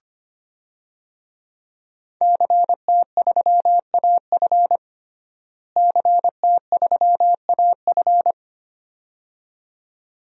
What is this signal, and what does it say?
Morse code, heard as a keyed tone, about 25 wpm, 700 Hz. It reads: CT3AF CT3AF
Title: DXCC-100 Call Signs - 5 Characters - Repeated 2x - 25wpm